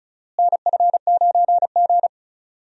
Morse code, heard as a keyed tone, about 36 wpm, 700 Hz. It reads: NF9G
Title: USA Call Signs - 4 Characters - Repeated 3x - 35wpm